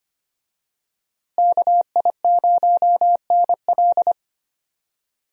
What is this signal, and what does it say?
Morse code, heard as a keyed tone, about 25 wpm, 700 Hz. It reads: KI0NL
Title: USA Call Signs - 5 Characters - Repeated 3x - 25wpm